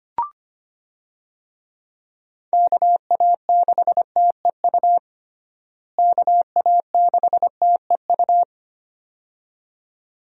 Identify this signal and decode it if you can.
Morse code, heard as a keyed tone, about 25 wpm, 700 Hz. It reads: KA6TEU KA6TEU
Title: USA Call Signs - 6 Characters - Repeated 2x - 25wpm